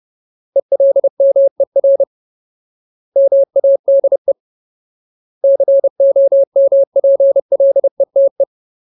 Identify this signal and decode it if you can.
Morse code, heard as a keyed tone, about 30 wpm, 550 Hz. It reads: ELMER MADE COMPLETE
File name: Ultimate Word List / 30WPM Ultimate